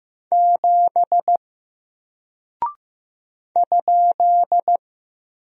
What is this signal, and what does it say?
Morse code, heard as a keyed tone, about 15 wpm, 700 Hz. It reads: 7 ?
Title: Single Character (Letter:Number:Punctuation:Prosign) 15wpm